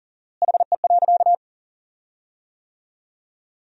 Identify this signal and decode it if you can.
Morse code, heard as a keyed tone, about 40 wpm, 700 Hz. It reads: HE.
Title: 4x Word-Spacing - Sentences from Top 100 Words - 40wpm